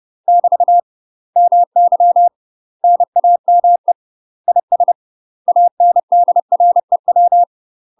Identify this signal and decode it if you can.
Morse code, heard as a keyed tone, about 30 wpm, 700 Hz. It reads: <BT> MY NAME IS ANDREW
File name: UK MASTER QSOs TRAINING / 30WPM / UK MASTER QSOs 0010